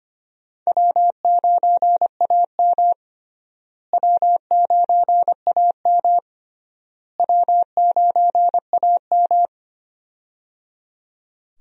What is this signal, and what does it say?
Morse code, heard as a keyed tone, about 25 wpm, 700 Hz. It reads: W9AM W9AM W9AM
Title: USA Call Signs - 4 Characters - Repeated 3x - 25wpm